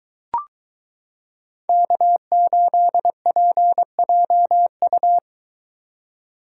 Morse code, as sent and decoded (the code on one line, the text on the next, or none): -.- ---.. .--. .--- ..-
K8PJU